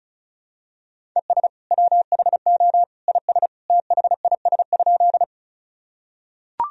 Morse code, heard as a keyed tone, about 36 wpm, 700 Hz. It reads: ES WHO IS THIS?
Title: Sentences from Top 100 Words and Abbreviations 35wpm